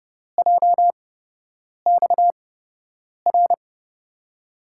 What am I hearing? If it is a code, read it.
Morse code, heard as a keyed tone, about 30 wpm, spaced as for 8 wpm, 700 Hz. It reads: JXR
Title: USA Call Signs - 5 Characters - 8x Character Spacing - 30wpm